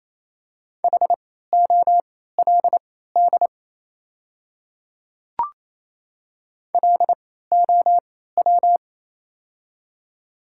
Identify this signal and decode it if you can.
Morse code, heard as a keyed tone, about 28 wpm, spaced as for 16 wpm, 690 Hz. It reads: HOLD LOW
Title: 3x Character Spacing - Top 500 Words - 28wpm